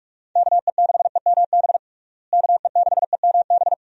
Morse code, heard as a keyed tone, about 46 wpm, 700 Hz. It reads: KE6EMB KE6EMB
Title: USA Call Signs - 6 Characters - Repeated 2x - 45wpm